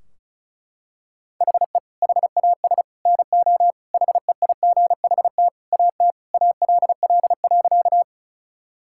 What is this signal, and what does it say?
Morse code, heard as a keyed tone, about 36 wpm, 700 Hz. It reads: HE HAS NO HEIGHT AT ALL.
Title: Sentences from Top 1600 Words 35wpm